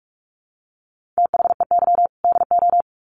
Morse code, heard as a keyed tone, about 44 wpm, 700 Hz. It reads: THEY DO
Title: Sets of 2 Words 45wpm